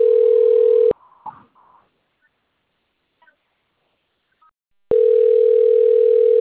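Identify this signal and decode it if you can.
Ringback tone (North America), 440 + 480 Hz.
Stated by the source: Outgoing phone call recorded on a Cornelio Procopio system.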